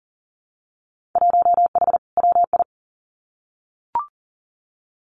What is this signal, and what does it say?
Morse code, heard as a keyed tone, about 38 wpm, 700 Hz. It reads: 1H WI
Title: Winter Field Day Exchanges 40wpm